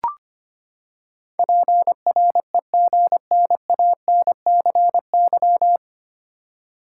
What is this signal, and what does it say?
Morse code, heard as a keyed tone, about 25 wpm, 700 Hz. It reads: PREGNANCY
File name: Nine Letter Words - 25wpm